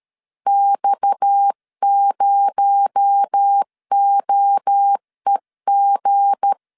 Morse code, heard as a keyed tone, about 13 wpm, 800 Hz. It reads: X0OEG
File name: LEZIONI / 7 GF9Z / 7 G 12wpm 2055